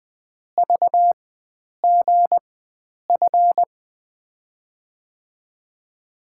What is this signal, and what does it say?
Morse code, heard as a keyed tone, about 20 wpm, spaced as for 9 wpm, 700 Hz. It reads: VGF
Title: USA Call Signs - 5 Characters - 4x Character Spacing - 20wpm